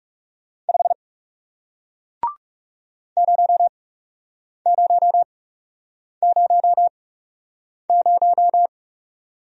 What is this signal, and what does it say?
Morse code, heard as a keyed tone, about 37 wpm, spaced as for 8 wpm, 700 Hz. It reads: 5 0000
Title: Speed-Racing Single Letter-Number 30wpm